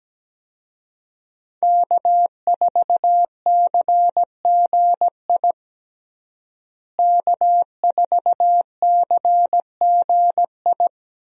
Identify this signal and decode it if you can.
Morse code, heard as a keyed tone, about 17 wpm, 700 Hz. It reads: K4CGI K4CGI
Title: USA Call Signs - 5 Characters - Repeated 3x - 17wpm